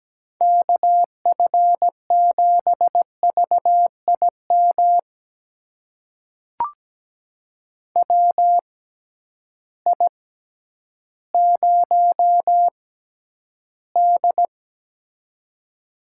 Morse code, heard as a keyed tone, about 17 wpm, spaced as for 10 wpm, 700 Hz. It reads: KF7VIM WI0D
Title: USA Call Signs - 6x Character Spacing - 17wpm